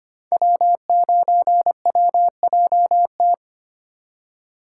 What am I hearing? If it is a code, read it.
Morse code, heard as a keyed tone, about 25 wpm, 700 Hz. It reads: W9WJT
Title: Code Groups - 5 Letter:Numbers - 25wpm